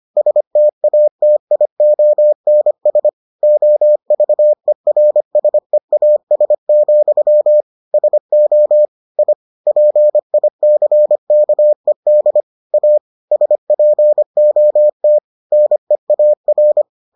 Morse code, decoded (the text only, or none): STATIONS OVERSEAS, SO I PICKED A SPOT NEAR